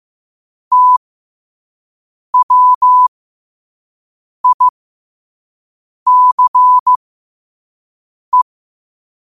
Morse code, heard as a keyed tone, about 15 wpm, spaced as for 5 wpm, 1000 Hz. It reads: TWICE
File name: qotdfiles / 5wpm